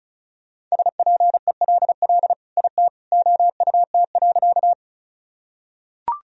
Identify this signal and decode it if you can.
Morse code, heard as a keyed tone, about 34 wpm, 700 Hz. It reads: SPELL IT OUT.
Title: Sentences from Top 1000 Words - Review Part 1 - 35wpm